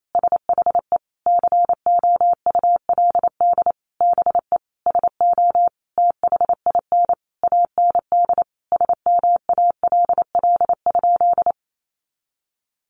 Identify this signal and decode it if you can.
Morse code, heard as a keyed tone, about 28 wpm, 700 Hz. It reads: SHE COULD BE SO THIN AND SMALL?
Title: Sentences from Top 1200 Words 28wpm